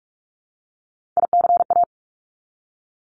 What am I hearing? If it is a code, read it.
Morse code, heard as a keyed tone, about 46 wpm, 700 Hz. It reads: ICA
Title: Top Three Letter Combinations in Words 45wpm